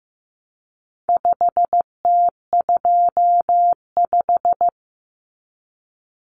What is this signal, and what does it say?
Morse code, heard as a keyed tone, about 15 wpm, 700 Hz. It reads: 5T25
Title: Common Cut Numbers - 4-Digit 15wpm